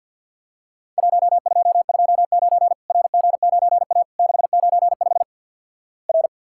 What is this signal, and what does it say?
Morse code, heard as a keyed tone, about 48 wpm, 700 Hz. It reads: 1229 RG9A 695 R
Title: Worked All Europe DX Contest - QTC Exchange - 50wpm